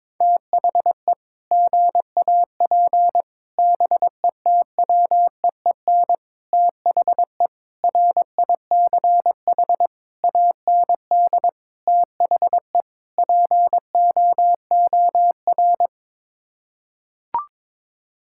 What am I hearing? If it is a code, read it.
Morse code, heard as a keyed tone, about 22 wpm, 700 Hz. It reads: THE GAP BETWEEN THE RICH AND THE POOR
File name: practice / 22wpm / Sets of 8 Words - 22wpm